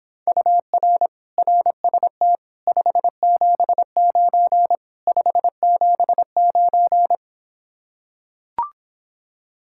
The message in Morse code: UR RST 579 579

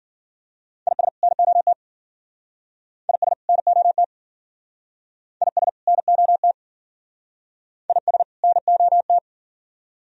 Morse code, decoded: IS NOT IS NOT IS NOT IS NOT

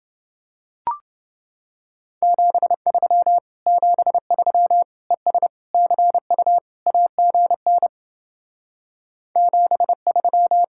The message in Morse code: 73 73 ES CU AGN 73

